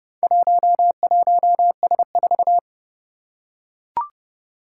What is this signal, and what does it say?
Morse code, heard as a keyed tone, about 30 wpm, 700 Hz. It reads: 11S4
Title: Four Challenging Letter-Number Combinations 30wpm